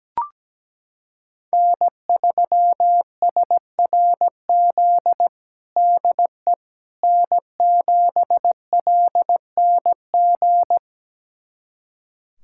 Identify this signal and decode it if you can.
Morse code, heard as a keyed tone, about 17 wpm, 700 Hz. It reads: N3SRZ DE N7LNG
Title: USA Callsign DE Callsign - 5 Characters - 17wpm